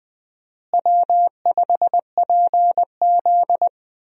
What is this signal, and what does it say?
Morse code, heard as a keyed tone, about 20 wpm, 700 Hz. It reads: W5PZ
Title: USA Call Signs - 4 Characters - Repeated 3x - 20wpm